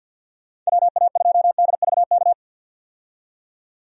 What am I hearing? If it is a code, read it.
Morse code, heard as a keyed tone, about 47 wpm, 700 Hz. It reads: WA2DVK
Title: USA Call Signs - 6 Characters - 50wpm